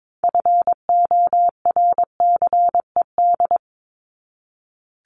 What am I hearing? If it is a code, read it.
Morse code, heard as a keyed tone, about 22 wpm, 700 Hz. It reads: FORCED
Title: ED Suffix - Top 100 Words with Base Words 22wpm